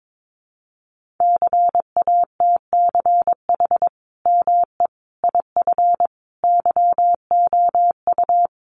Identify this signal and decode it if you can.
Morse code, heard as a keyed tone, about 22 wpm, 700 Hz. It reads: CATCH ME IF YOU
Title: Sentences from Top 1500 Words 22wpm